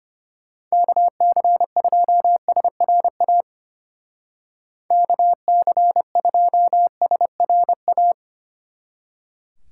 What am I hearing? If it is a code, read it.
Morse code, heard as a keyed tone, about 32 wpm, 700 Hz. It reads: KC2SRA KC2SRA